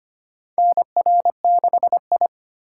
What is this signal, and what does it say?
Morse code, heard as a keyed tone, about 25 wpm, 700 Hz. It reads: NR6I
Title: USA Call Signs - 4 Characters - 4x Character Spacing - 25wpm